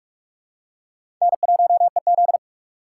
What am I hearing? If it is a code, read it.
Morse code, heard as a keyed tone, about 42 wpm, 700 Hz. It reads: N1EZ
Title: USA Call Signs - 4 Characters - 4x Character Spacing - 45wpm